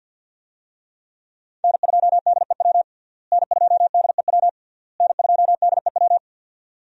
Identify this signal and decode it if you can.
Morse code, heard as a keyed tone, about 53 wpm, 700 Hz. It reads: N2DEW N2DEW N2DEW
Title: USA Call Signs - 5 Characters - Repeated 3x - 50wpm